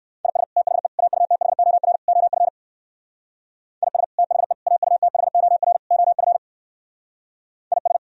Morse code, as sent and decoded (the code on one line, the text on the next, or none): .. ... / - .... / .- ..- - .... --- .-. / --- ..-. / .. ... / - .... / .- ..- - .... --- .-. / --- ..-. / .. ...
IS THE AUTHOR OF IS THE AUTHOR OF IS